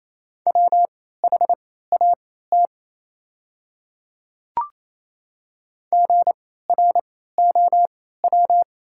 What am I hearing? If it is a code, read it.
Morse code, heard as a keyed tone, about 28 wpm, spaced as for 16 wpm, 700 Hz. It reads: WHAT GROW
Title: 3x Character Spacing - Top 300 Words - 28wpm